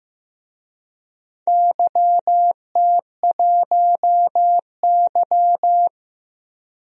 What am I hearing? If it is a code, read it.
Morse code, heard as a keyed tone, about 15 wpm, 700 Hz. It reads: YT1Y
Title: DXCC-100 Call Signs - 4 Characters - Repeated 2x - 15wpm